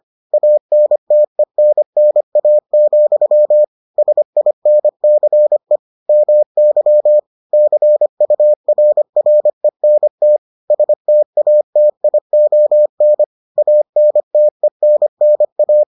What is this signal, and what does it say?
Morse code, heard as a keyed tone, about 25 wpm, 600 Hz. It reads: ANTENNA, SINCE MY CURRENT STATION ANTENNA